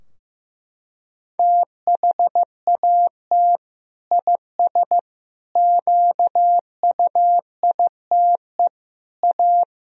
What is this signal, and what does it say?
Morse code, heard as a keyed tone, about 15 wpm, 700 Hz. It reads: THAT IS QUITE A
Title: Sentences from Top 1000 Words - Review Part 3 - 15wpm